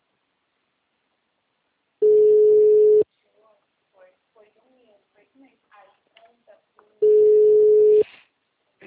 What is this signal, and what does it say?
Ringback tone (Europe, 425 Hz), with speech.
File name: out- -2008- - - .557